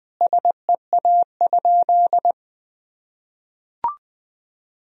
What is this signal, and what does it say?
Morse code, heard as a keyed tone, about 20 wpm, 700 Hz. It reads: SEA?
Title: Sentences from Top 1000 Words - Review Part 1 - 20wpm